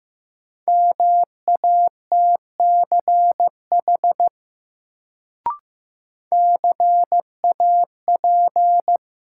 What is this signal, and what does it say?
Morse code, heard as a keyed tone, about 15 wpm, 700 Hz. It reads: MATCH CAP
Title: Sentences from Top 1600 Words - New Word Drill Alt - 15wpm